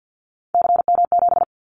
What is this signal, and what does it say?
Morse code, heard as a keyed tone, about 50 wpm, 700 Hz. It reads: CM7
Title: Callsign Prefixes for Top 100 DXCC - 50wpm